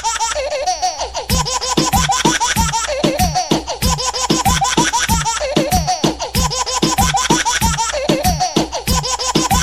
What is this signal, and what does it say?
There is no code.